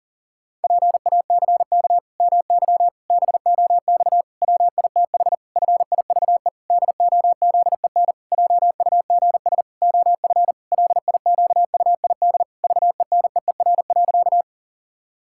Morse code, heard as a keyed tone, about 40 wpm, 700 Hz. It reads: PACK MY BOX WITH FIVE DOZEN JUGS OF LIQUID VENEER.
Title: Pangram Challenge v2 - 40wpm